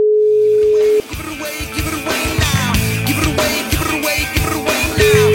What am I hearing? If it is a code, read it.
Ringback tone (Europe, 425 Hz), with noise.